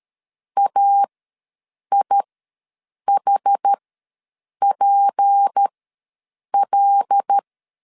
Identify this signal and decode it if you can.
Morse code, heard as a keyed tone, about 13 wpm, spaced as for 7 wpm, 800 Hz. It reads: AIHPL
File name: LEZIONI / 6 PKX0 / 6 P ws 12wpm 3117